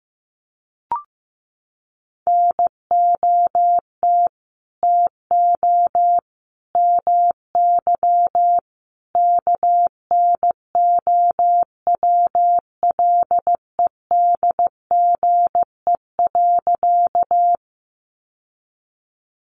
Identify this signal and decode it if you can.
Morse code, heard as a keyed tone, about 15 wpm, 700 Hz. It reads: NOT TO MY KNOWLEDGE.
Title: Sentences from Top 600 Words 15wpm